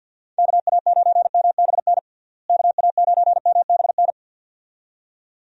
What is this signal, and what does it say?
Morse code, heard as a keyed tone, about 49 wpm, 700 Hz. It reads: KA9MBN KA9MBN
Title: USA Call Signs - 6 Characters - Repeated 2x - 50wpm